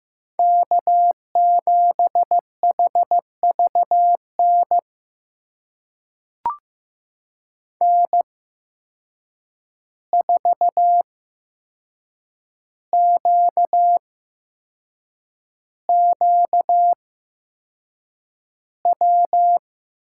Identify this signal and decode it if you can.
Morse code, heard as a keyed tone, about 15 wpm, spaced as for 6 wpm, 700 Hz. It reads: K7HVN N4QQW
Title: USA Call Signs - 5 Characters - 8x Character Spacing - 15wpm